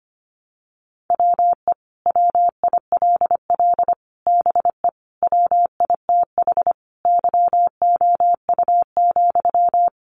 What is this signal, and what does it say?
Morse code, heard as a keyed tone, about 25 wpm, 700 Hz. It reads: WE WILL BE WITH YOU,
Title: Sentences from Top 600 Words 25wpm